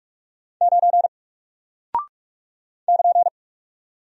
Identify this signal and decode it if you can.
Morse code, heard as a keyed tone, about 44 wpm, 700 Hz. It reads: 9 <KN>
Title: Single Character (Letter:Number:Punctuation:Prosign) Alt 45wpm